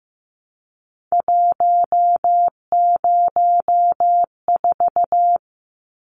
Morse code, heard as a keyed tone, about 15 wpm, 700 Hz. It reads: 104
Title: Numbers - 3-Digit 15wpm